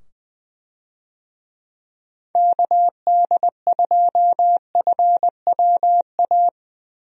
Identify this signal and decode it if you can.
Morse code, heard as a keyed tone, about 21 wpm, 700 Hz. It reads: KD2FWA
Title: USA Call Signs - 6 Characters - 20wpm